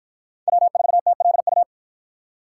Morse code, heard as a keyed tone, about 48 wpm, 700 Hz. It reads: W4TLU